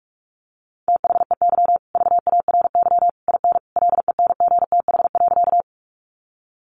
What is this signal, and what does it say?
Morse code, heard as a keyed tone, about 44 wpm, 700 Hz. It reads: THEY VARY IN LENGTH.
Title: Sentences from Top 1200 Words 45wpm